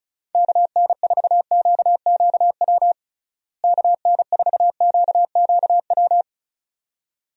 Morse code, heard as a keyed tone, about 34 wpm, 700 Hz. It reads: KN4QQW KN4QQW
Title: USA Call Signs - 6 Characters - Repeated 2x - 35wpm